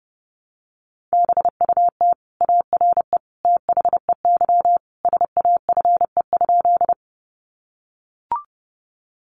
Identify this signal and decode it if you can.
Morse code, heard as a keyed tone, about 30 wpm, 700 Hz. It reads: BUT ARE THEY SAFE?
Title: Sentences from Top 900 Words 30wpm